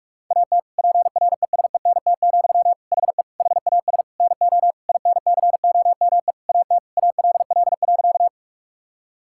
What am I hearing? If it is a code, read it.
Morse code, heard as a keyed tone, about 46 wpm, 700 Hz. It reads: AT PRESENT, HE HAS NO INCOME AT ALL.
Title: Sentences from Top 1000 Words - Review Part 5 - 45wpm